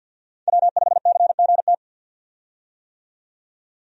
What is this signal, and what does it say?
Morse code, heard as a keyed tone, about 50 wpm, 700 Hz. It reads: W5CGT
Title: USA Call Signs - 5 Characters - 50wpm